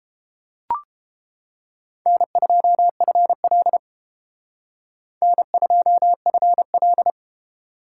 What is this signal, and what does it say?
Morse code, heard as a keyed tone, about 30 wpm, 700 Hz. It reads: N2FL N2FL